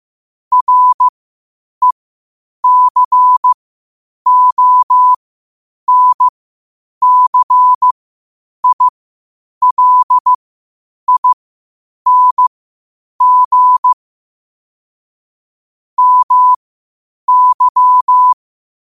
Morse code, heard as a keyed tone, about 15 wpm, spaced as for 8 wpm, 1000 Hz. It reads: RECONCILING MY
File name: qotdfiles / 8wpm